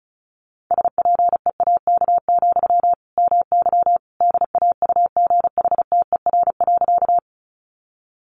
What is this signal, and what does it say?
Morse code, heard as a keyed tone, about 34 wpm, 700 Hz. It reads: SPEAK, MY DAUGHTER.
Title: Sentences from Top 1500 Words 35wpm